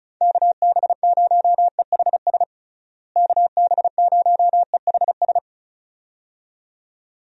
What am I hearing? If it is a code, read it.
Morse code, heard as a keyed tone, about 36 wpm, 700 Hz. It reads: KB0EHS KB0EHS
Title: USA Call Signs - 6 Characters - Repeated 2x - 35wpm